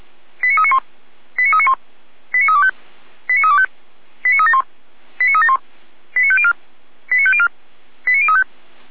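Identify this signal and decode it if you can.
Five-tone selective call (ZVEI), heard as quick runs of five tones, tones of about 70 ms. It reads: 89391 89391 89325 89325 89371 89371 89504 89504 89035
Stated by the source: ZVEI five-tone selective call 89125, received by SDR.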